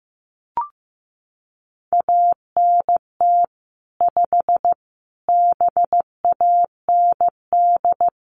Morse code, heard as a keyed tone, about 15 wpm, 700 Hz. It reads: ANT 5 BAND
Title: Common QSO Phrases 15wpm